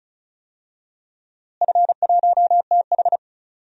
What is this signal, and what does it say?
Morse code, heard as a keyed tone, about 36 wpm, 700 Hz. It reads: F1TH
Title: DXCC-100 Call Signs - 4 Characters - Repeated 3x - 35wpm